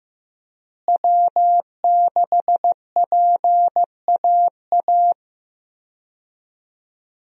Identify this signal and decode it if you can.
Morse code, heard as a keyed tone, about 14 wpm, 700 Hz. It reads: W6PAA
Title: USA Call Signs - 5 Characters - Repeated 2x - 15wpm